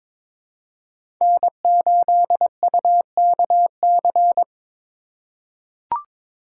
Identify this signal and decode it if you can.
Morse code, heard as a keyed tone, about 22 wpm, 700 Hz. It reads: N8UKC
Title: USA Call Signs - 8x Character Spacing - 22wpm